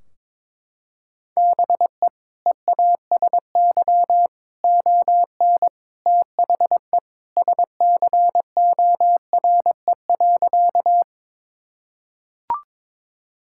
Morse code, heard as a keyed tone, about 22 wpm, 700 Hz. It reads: BE EASY ON THE SCORE.
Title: Sentences from Top 1600 Words 22wpm